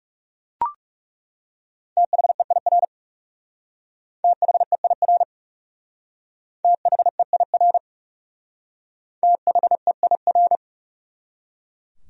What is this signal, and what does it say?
Morse code, heard as a keyed tone, about 39 wpm, 700 Hz. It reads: THEIR THEIR THEIR THEIR